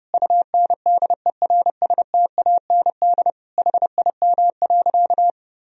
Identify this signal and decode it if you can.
Morse code, heard as a keyed tone, about 30 wpm, 700 Hz. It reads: UNDERSTAND HIM.
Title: Sentences from Top 1000 Words - Review Part 1 - 30wpm